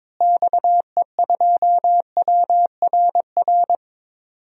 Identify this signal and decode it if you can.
Morse code, heard as a keyed tone, about 22 wpm, 700 Hz. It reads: XE2WRR
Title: DXCC-100 Call Signs - 6 Characters - Repeated 2x - 22wpm